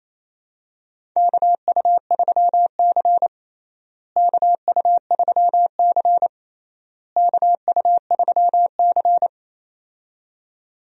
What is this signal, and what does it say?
Morse code, heard as a keyed tone, about 28 wpm, 700 Hz. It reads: KU3C KU3C KU3C
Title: USA Call Signs - 4 Characters - Repeated 3x - 28wpm